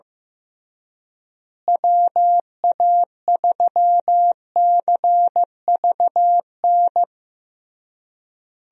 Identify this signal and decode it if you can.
Morse code, heard as a keyed tone, about 15 wpm, 700 Hz. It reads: WA3CVN
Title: USA Call Signs - 6 Characters - Repeated 3x - 15wpm